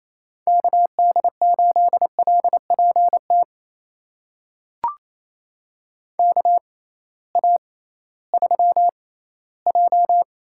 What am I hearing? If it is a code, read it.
Morse code, heard as a keyed tone, about 27 wpm, spaced as for 16 wpm, 700 Hz. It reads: KD8LPT KA3J
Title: USA Call Signs - 6 Characters - 6x Character Spacing - 28wpm